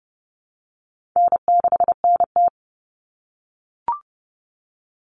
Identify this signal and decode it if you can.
Morse code, heard as a keyed tone, about 30 wpm, 700 Hz. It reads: N6NT
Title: Common Cut Numbers - 4-Digit 30wpm